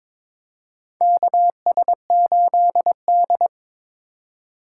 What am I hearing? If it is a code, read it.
Morse code, heard as a keyed tone, about 22 wpm, 700 Hz. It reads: KS8D
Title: USA Call Signs - 4 Characters - 4x Character Spacing - 22wpm